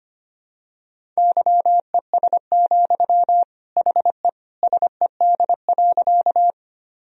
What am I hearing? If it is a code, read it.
Morse code, heard as a keyed tone, about 25 wpm, 700 Hz. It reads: YES, HE SED.
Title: Sentences from Top 500 Words and Abbreviations 25wpm